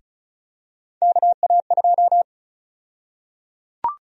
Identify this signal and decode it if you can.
Morse code, heard as a keyed tone, about 36 wpm, 700 Hz. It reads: KA2
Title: USA Call Sign Prefixes - Repeated 2x - 35wpm